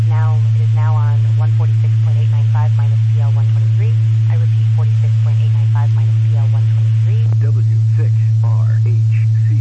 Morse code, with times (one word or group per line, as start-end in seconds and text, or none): none